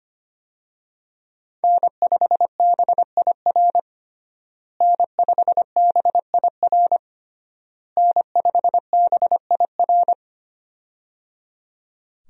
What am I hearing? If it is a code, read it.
Morse code, heard as a keyed tone, about 25 wpm, 700 Hz. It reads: N5BIR N5BIR N5BIR
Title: USA Call Signs - 5 Characters - Repeated 3x - 25wpm